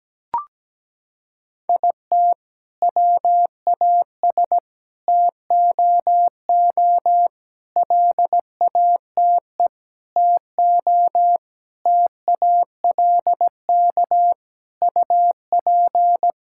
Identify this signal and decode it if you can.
Morse code, heard as a keyed tone, about 17 wpm, 700 Hz. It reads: IT WAS TOO LATE TO TALK UP
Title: Sentences from Top 1600 Words 17wpm